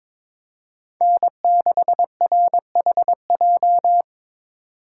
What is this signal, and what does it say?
Morse code, heard as a keyed tone, about 26 wpm, 700 Hz. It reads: N6RHJ
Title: Speed Racing - USA Call Signs 22wpm